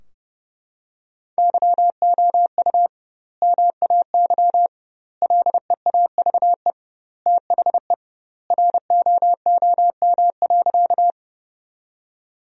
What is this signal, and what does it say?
Morse code, heard as a keyed tone, about 30 wpm, 700 Hz. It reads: YOU MAY LEAVE THE ROOM.
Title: 2x Word-Spacing - Sentences from Top 400 Words - 30wpm